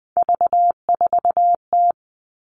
Morse code, heard as a keyed tone, about 20 wpm, 700 Hz. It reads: V4T
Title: All Cut Numbers - 3-Digit 20wpm